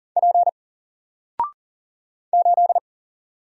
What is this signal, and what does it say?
Morse code, heard as a keyed tone, about 40 wpm, 700 Hz. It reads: P 8
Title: Single Letter-Number 40wpm